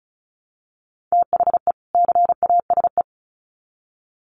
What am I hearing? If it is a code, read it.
Morse code, heard as a keyed tone, about 34 wpm, 700 Hz. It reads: THE CASE
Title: Sets of 2 Words 35wpm